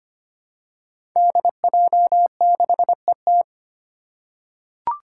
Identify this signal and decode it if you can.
Morse code, heard as a keyed tone, about 25 wpm, 700 Hz. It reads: DJ6ET
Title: Five Challenging Letter-Number Combinations 25wpm